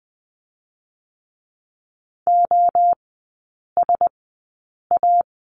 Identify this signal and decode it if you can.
Morse code, heard as a keyed tone, about 20 wpm, spaced as for 8 wpm, 700 Hz. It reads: OSA
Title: 035 Lesson 7e - Letter s - Three Characters 20wpm